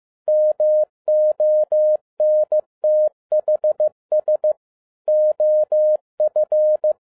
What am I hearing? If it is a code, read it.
Morse code, heard as a keyed tone, about 15 wpm, 610 Hz. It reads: MONTHS OF
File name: 2011-07-mp3 / n07 A LOT HAS CHANGED 15wpm